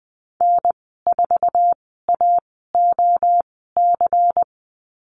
Morse code, heard as a keyed tone, about 20 wpm, spaced as for 15 wpm, 700 Hz. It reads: N4AOC
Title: 122 Lesson 20g - Number 3 - Call Signs 20wpm